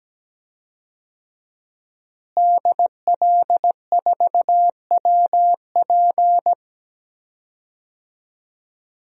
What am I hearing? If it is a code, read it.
Morse code, heard as a keyed tone, about 17 wpm, 700 Hz. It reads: DL4WP
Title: DXCC-100 Call Signs - 5 Characters - Repeated 2x - 17wpm